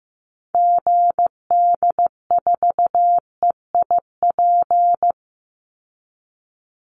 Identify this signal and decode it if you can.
Morse code, heard as a keyed tone, about 15 wpm, 700 Hz. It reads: GD4EIP
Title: Call Signs Repeated 3x - 15wpm